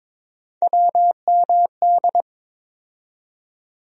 Morse code, heard as a keyed tone, about 22 wpm, 700 Hz. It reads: WMD
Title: USA Call Sign Postfixes - 3 Letters 22wpm